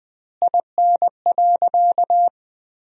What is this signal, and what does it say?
Morse code, heard as a keyed tone, about 21 wpm, 700 Hz. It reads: IN.
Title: 2x Word-Spacing - Sentences from Top 500 Words - 20wpm